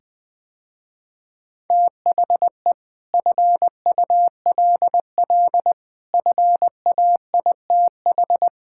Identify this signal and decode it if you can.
Morse code, heard as a keyed tone, about 21 wpm, 700 Hz. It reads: THE FULL FAITH
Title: Sentences with 8-Word Phrases - Spoken Beginning - 20wpm